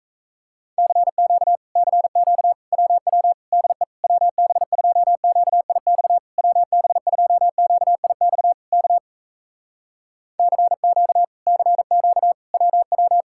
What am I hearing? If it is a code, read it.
Morse code, heard as a keyed tone, about 38 wpm, 700 Hz. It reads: CQ CQ WW DE WB2QIX WB2QIX K CQ CQ WW